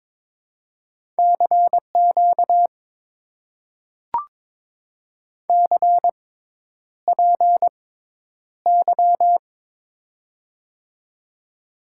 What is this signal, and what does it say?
Morse code, heard as a keyed tone, about 22 wpm, spaced as for 10 wpm, 700 Hz. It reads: CQ CPY
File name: Top 90 QSO Elements - 6x Character Spacing - 22wpm